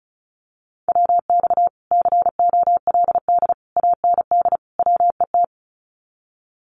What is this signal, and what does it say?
Morse code, heard as a keyed tone, about 34 wpm, 700 Hz. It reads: WX COLD AND WET
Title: Common QSO Phrases Encore 35wpm